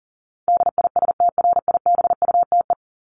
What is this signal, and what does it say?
Morse code, heard as a keyed tone, about 40 wpm, 700 Hz. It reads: DISTRIBUTE
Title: Top 1501-1750 Words 40wpm